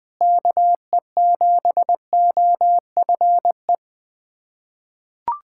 Morse code, heard as a keyed tone, about 20 wpm, 700 Hz. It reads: KE7OFE
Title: USA Callsign DE Callsign - 6 Characters - 20wpm